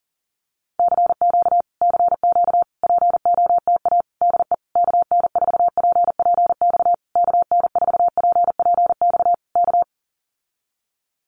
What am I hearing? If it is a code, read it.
Morse code, heard as a keyed tone, about 40 wpm, 700 Hz. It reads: CQ CQ POTA DE KN4PPX KN4PPX K
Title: Common QSO Phrases Encore 40wpm